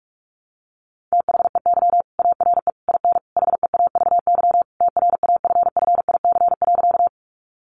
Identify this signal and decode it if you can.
Morse code, heard as a keyed tone, about 44 wpm, 700 Hz. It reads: THEY ARE IN HEAVY TRAFFIC.
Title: Sentences from Top 1300 Words 45wpm